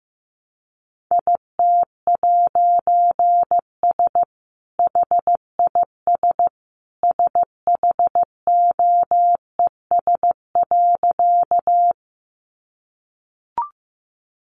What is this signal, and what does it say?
Morse code, heard as a keyed tone, about 14 wpm, 700 Hz. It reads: IT'S HIS SHOES.
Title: A Sound Of Thunder 15wpm